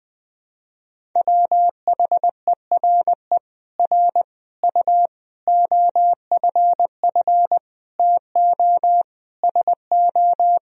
Morse code, heard as a keyed tone, about 21 wpm, 700 Hz. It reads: WHERE R U OFF TO SO
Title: Sentences from Top 400 Words and Abbreviations 20wpm